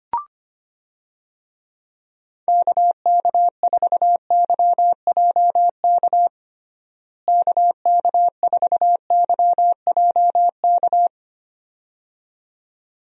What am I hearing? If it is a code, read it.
Morse code, heard as a keyed tone, about 25 wpm, 700 Hz. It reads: KK4YJK KK4YJK
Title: USA Call Signs - 6 Characters - Repeated 2x - 25wpm